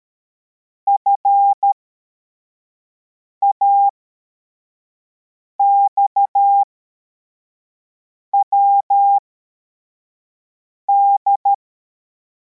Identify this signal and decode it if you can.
Morse code, heard as a keyed tone, about 13 wpm, spaced as for 4 wpm, 800 Hz. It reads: FAXWD
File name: formation / cw / Text85